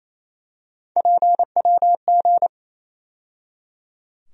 Morse code, heard as a keyed tone, about 28 wpm, 700 Hz. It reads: PWG
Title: USA Call Sign Postfixes - 3 Letters 28wpm